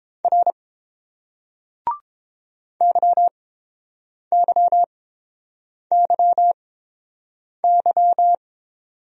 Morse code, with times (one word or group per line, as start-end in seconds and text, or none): 0.2-0.5 R
2.8-8.3 YYYY